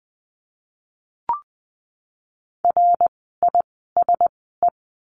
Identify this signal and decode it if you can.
Morse code, heard as a keyed tone, about 20 wpm, spaced as for 15 wpm, 700 Hz. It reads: RISE
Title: 054 Lesson 10f - Letter r - Words 20wpm